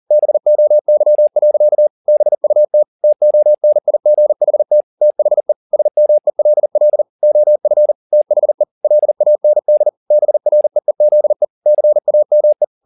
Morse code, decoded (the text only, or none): BOY. BUT TONIGHT THE SMELL OF THE LAND BREEZE CAME